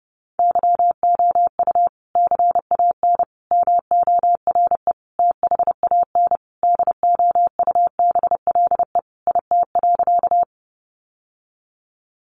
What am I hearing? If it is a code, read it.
Morse code, heard as a keyed tone, about 30 wpm, 700 Hz. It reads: YOU CAN MORE THAN DOUBLE IT.
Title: Sentences from Top 1400 Words 30wpm